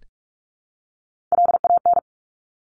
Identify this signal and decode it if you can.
Morse code, heard as a keyed tone, about 46 wpm, 700 Hz. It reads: LAN
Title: Top Three Letter Combinations in Words 45wpm